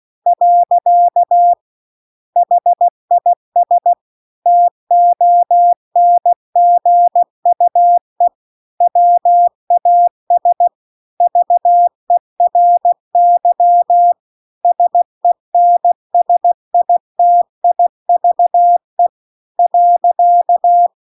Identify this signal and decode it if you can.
Morse code, heard as a keyed tone, about 16 wpm, 700 Hz. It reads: . HIS TONGUE WAS VERY SENSITIVE .